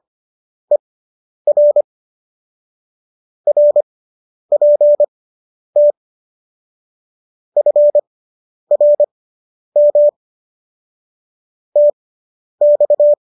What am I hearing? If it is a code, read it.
Morse code, heard as a keyed tone, about 25 wpm, spaced as for 10 wpm, 590 Hz. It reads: ER RPT FRM TX